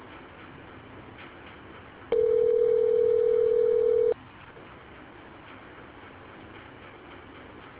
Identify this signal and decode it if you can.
Ringback tone (North America), 440 + 480 Hz.